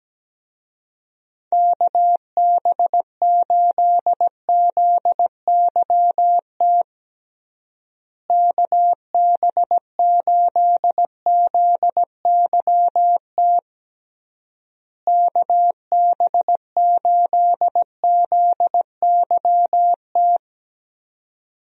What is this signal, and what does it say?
Morse code, heard as a keyed tone, about 17 wpm, 700 Hz. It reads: KB8ZYT KB8ZYT KB8ZYT
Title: USA Call Signs - 6 Characters - Repeated 3x - 17wpm